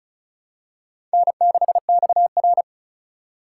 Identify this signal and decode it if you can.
Morse code, heard as a keyed tone, about 36 wpm, 700 Hz. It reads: N6XR
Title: USA Call Signs - 4 Characters - 4x Character Spacing - 35wpm